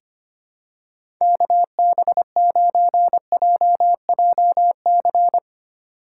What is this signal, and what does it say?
Morse code, heard as a keyed tone, about 25 wpm, 700 Hz. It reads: KB9JJC
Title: USA Call Signs - 8x Character Spacing - 25wpm